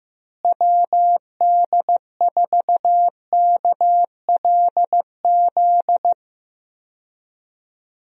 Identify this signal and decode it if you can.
Morse code, heard as a keyed tone, about 15 wpm, 700 Hz. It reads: WD4KLZ
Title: USA Call Signs - 6 Characters - Repeated 3x - 15wpm